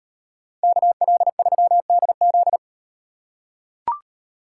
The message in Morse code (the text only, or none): KL3DZ